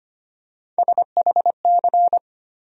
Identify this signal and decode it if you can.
Morse code, heard as a keyed tone, about 25 wpm, 700 Hz. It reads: SHC
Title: USA Call Sign Postfixes - 3 Letters 25wpm